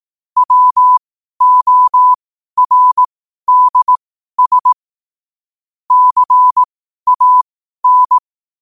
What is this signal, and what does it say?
Morse code, heard as a keyed tone, about 18 wpm, spaced as for 12 wpm, 1000 Hz. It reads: WORDS CAN
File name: qotdfiles / 12wpm